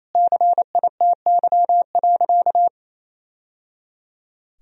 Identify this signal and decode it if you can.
Morse code, heard as a keyed tone, about 28 wpm, 700 Hz. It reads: CITY.
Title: Sentences from Top 1000 Words - Review Part 1 - 28wpm